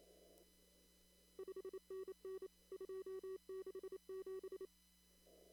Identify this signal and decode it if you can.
Morse code, heard as a keyed tone, about 27 wpm, 380 Hz. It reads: 5NN 267
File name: CW / kt5j wpx cw-op-k5pi